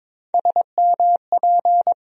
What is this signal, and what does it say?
Morse code, heard as a keyed tone, about 22 wpm, 700 Hz. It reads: SMP
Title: USA Call Sign Postfixes - 3 Letters 22wpm